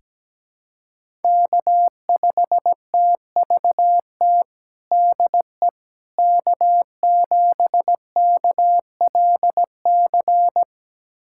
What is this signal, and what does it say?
Morse code, heard as a keyed tone, about 17 wpm, 700 Hz. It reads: K5TVT DE K7KLC
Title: USA Callsign DE Callsign - 5 Characters - 17wpm